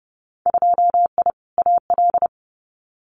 Morse code, heard as a keyed tone, about 30 wpm, 700 Hz. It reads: 2I AL
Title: Winter Field Day Exchanges 30wpm